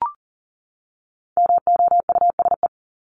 Morse code, heard as a keyed tone, about 40 wpm, 700 Hz. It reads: MOUSE
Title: Top 501-750 Words 40wpm